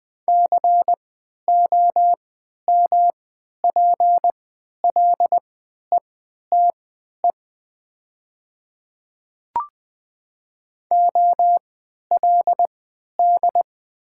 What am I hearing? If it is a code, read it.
Morse code, heard as a keyed tone, about 20 wpm, spaced as for 11 wpm, 700 Hz. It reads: COMPLETE OLD
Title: 3x Character Spacing - Top 500 Words - 20wpm